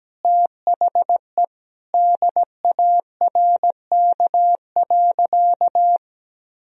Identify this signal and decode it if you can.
Morse code, heard as a keyed tone, about 17 wpm, 700 Hz. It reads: THE DARK.
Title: Sentences from Top 1000 Words - Review Part 4 - 17wpm